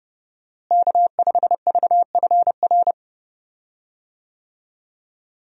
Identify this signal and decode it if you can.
Morse code, heard as a keyed tone, about 30 wpm, 700 Hz. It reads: K5VFR
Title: USA Call Signs - 5 Characters - Repeated 2x - 30wpm